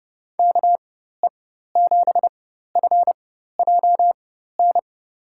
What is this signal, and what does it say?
Morse code, heard as a keyed tone, about 30 wpm, spaced as for 14 wpm, 700 Hz. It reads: KE7FJN
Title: USA Call Signs - 6 Characters - 4x Character Spacing - 30wpm